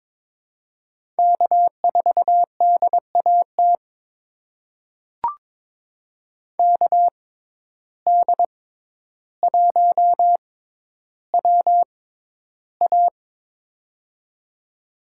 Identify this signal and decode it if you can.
Morse code, heard as a keyed tone, about 22 wpm, spaced as for 11 wpm, 700 Hz. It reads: K4DAT KD1WA
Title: USA Call Signs - 5 Characters - 6x Character Spacing - 22wpm